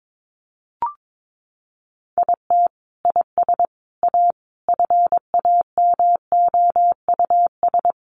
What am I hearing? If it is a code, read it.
Morse code, heard as a keyed tone, about 22 wpm, 700 Hz. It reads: IT IS A FAMOUS
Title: Sentences from Top 1500 Words 22wpm